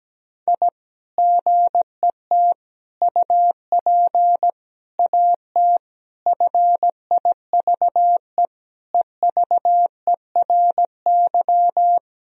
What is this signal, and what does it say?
Morse code, heard as a keyed tone, about 17 wpm, 700 Hz. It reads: I GET UP AT FIVE EVERY
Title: Sentences from Top 1000 Words - Review Part 2 - 17wpm